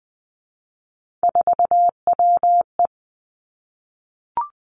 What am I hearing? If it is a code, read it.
Morse code, heard as a keyed tone, about 20 wpm, 700 Hz. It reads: 4WE
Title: Three Challenging Letter-Number Combinations 20wpm